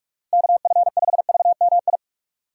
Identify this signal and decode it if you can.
Morse code, heard as a keyed tone, about 46 wpm, 700 Hz. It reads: KF5VMI
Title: USA Call Signs - 6 Characters - 3x Character Spacing - 45wpm